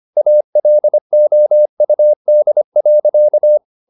Morse code, heard as a keyed tone, about 25 wpm, 600 Hz. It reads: ALOUD.